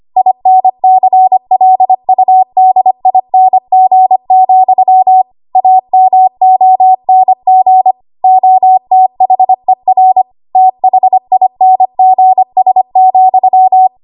Morse code, heard as a keyed tone, about 25 wpm, 750 Hz. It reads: INCLUDING, AMONG OTHER THINGS,
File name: Archive / 25 WPM / 25WPM